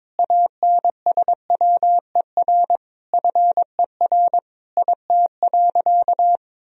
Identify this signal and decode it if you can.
Morse code, heard as a keyed tone, about 22 wpm, 700 Hz. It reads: ANSWER FER IT.
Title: Sentences from Top 300 Words and Abbreviations 22wpm